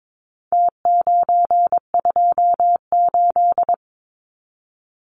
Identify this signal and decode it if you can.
Morse code, heard as a keyed tone, about 22 wpm, 700 Hz. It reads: T928
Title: Numbers - Contest Style - 4-Digit 22wpm